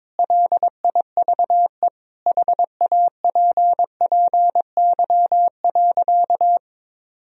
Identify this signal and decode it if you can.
Morse code, heard as a keyed tone, about 22 wpm, 700 Hz. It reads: LIVE HAPPY.
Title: Sentences from Top 1000 Words - Review Part 3 - 22wpm